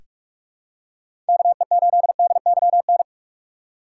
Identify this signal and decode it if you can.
Morse code, heard as a keyed tone, about 41 wpm, 700 Hz. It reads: KE8DYN
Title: USA Call Signs - 3x Character Spacing - 45wpm